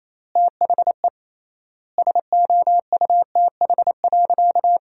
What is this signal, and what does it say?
Morse code, heard as a keyed tone, about 28 wpm, 700 Hz. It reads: THE SOUTH.
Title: 3x Word-Spacing - Sentences from Top 400 Words - 28wpm